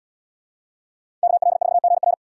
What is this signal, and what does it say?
Morse code, heard as a keyed tone, about 69 wpm, 700 Hz. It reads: KL4GR